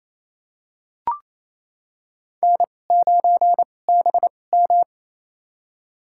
Morse code, heard as a keyed tone, about 28 wpm, spaced as for 20 wpm, 700 Hz. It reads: N9BM
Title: USA Call Signs - 4 Characters - 2x Character Spacing - 28wpm